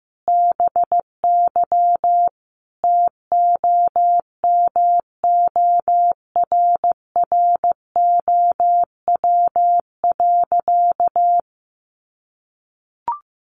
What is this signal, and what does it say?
Morse code, heard as a keyed tone, about 15 wpm, 700 Hz. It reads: BY TOMORROW.
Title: Sentences from Top 1100 Words 15wpm